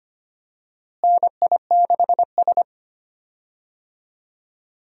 Morse code, heard as a keyed tone, about 25 wpm, 700 Hz. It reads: NI6S
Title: USA Call Signs - 4 Characters - Repeated 3x - 25wpm